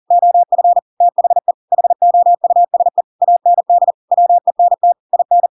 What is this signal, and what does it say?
Morse code, heard as a keyed tone, about 40 wpm, 700 Hz. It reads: OF THE HOUSE AND WENT IN
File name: Pinocchio / Pinocchio ch7 40 40 ik1hge